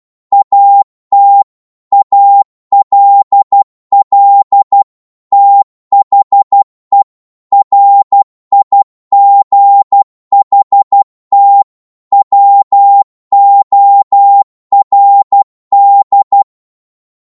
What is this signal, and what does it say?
Morse code, heard as a keyed tone, about 12 wpm, 800 Hz. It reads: AT ALL THE RIGHT WORD
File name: Morse / psg-12wpm